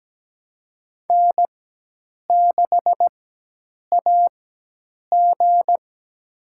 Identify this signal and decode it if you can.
Morse code, heard as a keyed tone, about 17 wpm, spaced as for 8 wpm, 700 Hz. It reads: N6AG
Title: USA Call Signs - 4x Character Spacing - 17wpm